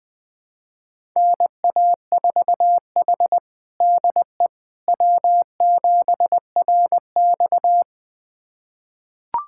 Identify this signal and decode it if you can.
Morse code, heard as a keyed tone, about 19 wpm, 700 Hz. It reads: NA4H DE W7RX
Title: USA Callsign DE Callsign - 4 Characters - 20wpm